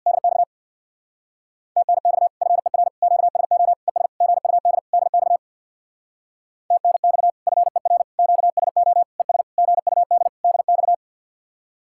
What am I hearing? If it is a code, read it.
Morse code, heard as a keyed tone, about 66 wpm, 700 Hz. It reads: DX TNX FER QSO ES GUD DX TNX FER QSO ES GUD DX